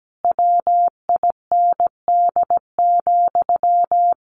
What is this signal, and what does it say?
Morse code, heard as a keyed tone, about 17 wpm, 700 Hz. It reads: WIND,
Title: Sentences from Top 1400 Words 17wpm